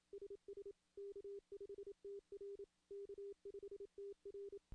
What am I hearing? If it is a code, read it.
Morse code, heard as a keyed tone, about 26 wpm, 400 Hz. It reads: SS K5TR K5TR
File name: CW / CW-SS-k5tr- -1700